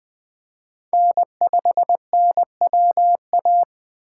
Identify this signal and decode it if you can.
Morse code, heard as a keyed tone, about 21 wpm, 700 Hz. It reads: N5NWA
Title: Speed Racing - USA Call Signs 17wpm